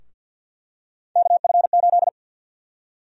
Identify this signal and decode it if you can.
Morse code, heard as a keyed tone, about 51 wpm, 700 Hz. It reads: KF8
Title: USA Call Sign Prefixes - Repeated 2x - 50wpm